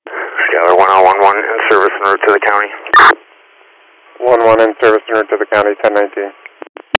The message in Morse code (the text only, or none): none